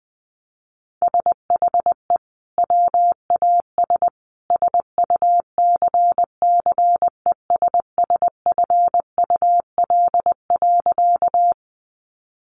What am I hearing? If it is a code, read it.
Morse code, heard as a keyed tone, about 19 wpm, 700 Hz. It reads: SHE WAS SUCCESSFUL.
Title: Sentences from Top 900 Words 20wpm